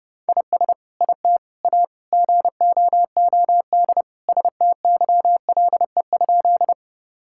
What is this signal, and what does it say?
Morse code, heard as a keyed tone, about 30 wpm, 700 Hz. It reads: IS IT A GOOD STYLE?
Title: Sentences from Top 1000 Words - Review Part 5 - 30wpm